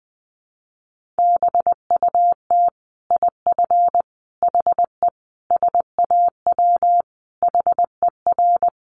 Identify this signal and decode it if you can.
Morse code, heard as a keyed tone, about 20 wpm, 700 Hz. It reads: BUT IF HE SAW HER
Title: Sentences from Top 1400 Words 20wpm